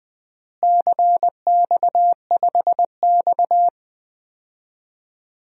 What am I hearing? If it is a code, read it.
Morse code, heard as a keyed tone, about 20 wpm, 700 Hz. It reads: CX5X
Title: DXCC-100 Call Signs - 4 Characters - Repeated 2x - 20wpm